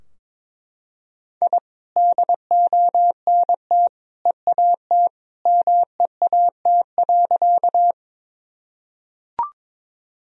Morse code, heard as a keyed tone, about 21 wpm, 700 Hz. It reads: I DONT EAT MEAT.
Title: Sentences from Top 1000 Words - Review Part 3 - 22wpm